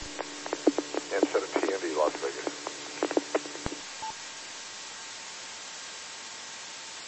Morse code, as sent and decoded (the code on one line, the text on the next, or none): none